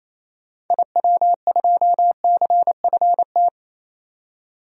Morse code, heard as a keyed tone, about 28 wpm, 700 Hz. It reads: IW2CFT
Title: DXCC-100 Call Signs - 6 Characters - 28wpm